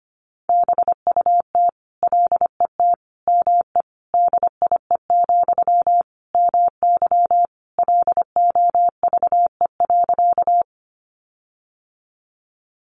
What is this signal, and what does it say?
Morse code, heard as a keyed tone, about 25 wpm, 700 Hz. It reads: BUT LET ME DIE, MY LOVE.
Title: Sentences from Top 1300 Words 25wpm